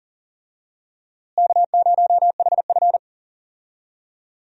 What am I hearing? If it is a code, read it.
Morse code, heard as a keyed tone, about 37 wpm, 700 Hz. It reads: K0HF
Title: USA Call Signs - 4 Characters - Repeated 2x - 40wpm